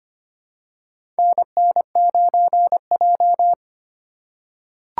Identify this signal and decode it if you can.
Morse code, heard as a keyed tone, about 25 wpm, 700 Hz. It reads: NN9J
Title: USA Call Signs - 4 Characters - 3x Character Spacing - 25wpm